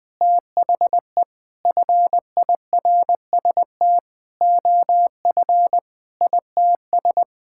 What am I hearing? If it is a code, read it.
Morse code, heard as a keyed tone, about 21 wpm, 700 Hz. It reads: THE FIRST OF ITS
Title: Sentences with 8-Word Phrases - 20wpm